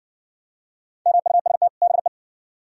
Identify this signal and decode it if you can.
Morse code, heard as a keyed tone, about 59 wpm, 700 Hz. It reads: MUST BE